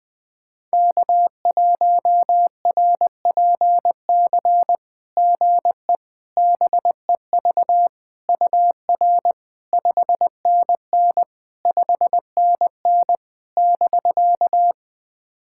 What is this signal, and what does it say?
Morse code, heard as a keyed tone, about 20 wpm, 690 Hz. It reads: K1RPC GE BEV UR 5NN 5NN <BK>
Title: SOTA QSO Practice 20wpm